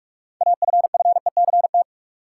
Morse code, heard as a keyed tone, about 46 wpm, 700 Hz. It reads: AFFECT
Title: Sentences from Top 900 Words - New Word Drill - 45wpm